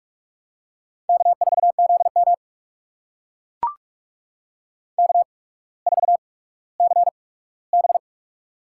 Morse code, heard as a keyed tone, about 44 wpm, spaced as for 19 wpm, 700 Hz. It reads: K4ZM K4CB
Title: USA Call Signs - 4 Characters - 8x Character Spacing - 45wpm